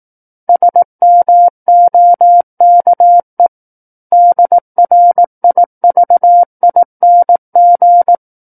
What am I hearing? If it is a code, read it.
Morse code, heard as a keyed tone, about 18 wpm, 700 Hz. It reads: SMOKE DRIVING
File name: 18 WPM